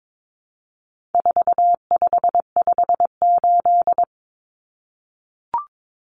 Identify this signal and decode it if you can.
Morse code, heard as a keyed tone, about 22 wpm, 700 Hz. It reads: 4558
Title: Numbers - 4-Digit 22wpm